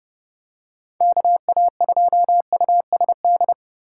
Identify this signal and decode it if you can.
Morse code, heard as a keyed tone, about 29 wpm, 700 Hz. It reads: KA2USD